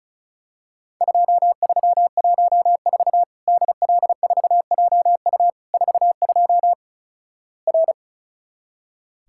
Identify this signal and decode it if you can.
Morse code, heard as a keyed tone, about 36 wpm, 700 Hz. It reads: 2314 DL4JU 42 R
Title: Worked All Europe DX Contest - QTC Exchange - 35wpm